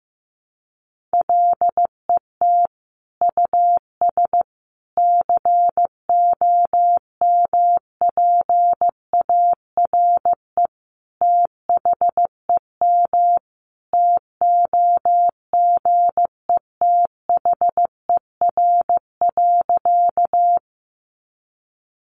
Morse code, heard as a keyed tone, about 15 wpm, 700 Hz. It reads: LET US COMPARE THEM TOGETHER.
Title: Sentences from Top 1300 Words 15wpm